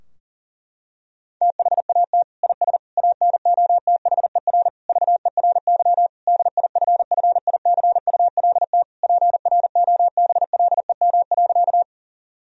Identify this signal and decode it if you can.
Morse code, heard as a keyed tone, about 37 wpm, 700 Hz. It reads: THAT IS ANOTHER VERY DIFFICULT PROBLEM.
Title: Sentences from Top 1000 Words - Review Part 2 - 40wpm